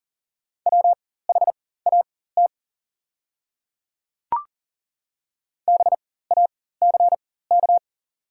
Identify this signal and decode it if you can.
Morse code, heard as a keyed tone, about 40 wpm, spaced as for 19 wpm, 700 Hz. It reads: WHAT BACK
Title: 4x Character Spacing - Top 200 Words - 40wpm